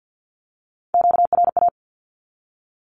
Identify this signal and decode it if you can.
Morse code, heard as a keyed tone, about 48 wpm, 700 Hz. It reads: QRA
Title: Q-Codes 50wpm